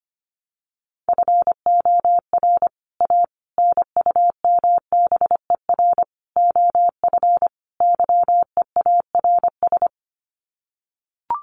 Morse code, heard as a keyed tone, about 25 wpm, 700 Hz. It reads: FOR A NUMBER OF YEARS
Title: Sets of 5 Words Encore 25wpm